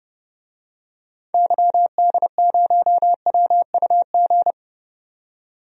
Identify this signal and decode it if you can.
Morse code, heard as a keyed tone, about 30 wpm, 700 Hz. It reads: YD0WUG
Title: DXCC-100 Call Signs - 6 Characters - 30wpm